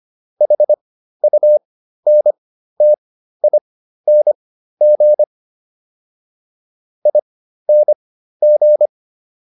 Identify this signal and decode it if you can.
Morse code, heard as a keyed tone, about 25 wpm, spaced as for 13 wpm, 600 Hz. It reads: HUNTING ING